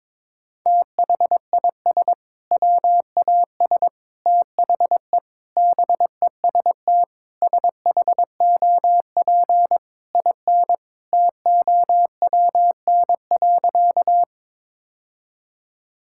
Morse code, heard as a keyed tone, about 22 wpm, 700 Hz. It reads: THIS WAS THE BEST SHOP IN TOWN.
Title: Sentences from Top 1600 Words 22wpm